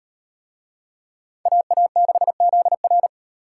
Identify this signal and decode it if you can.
Morse code, heard as a keyed tone, about 33 wpm, 700 Hz. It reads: AA6ZR